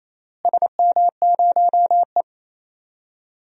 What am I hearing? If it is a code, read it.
Morse code, heard as a keyed tone, about 28 wpm, 700 Hz. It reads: SM0E
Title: DXCC-100 Call Signs - 4 Characters - Repeated 3x - 28wpm